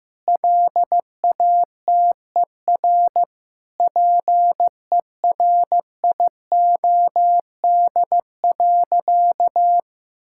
Morse code, heard as a keyed tone, about 15 wpm, 700 Hz. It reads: LATER PERIOD.
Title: Sentences from Top 1000 Words - Review Part 5 - 15wpm